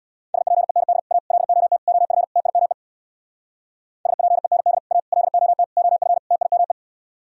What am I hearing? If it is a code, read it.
Morse code, heard as a keyed tone, about 64 wpm, 700 Hz. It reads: SPEND A LOT OF TIME SPEND A LOT OF TIME